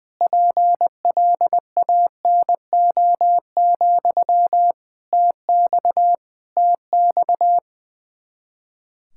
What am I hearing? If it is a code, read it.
Morse code, heard as a keyed tone, about 21 wpm, 700 Hz. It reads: PLANO, TX TX
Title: Practice QSO - Exchange 1 - Beginning - 20wpm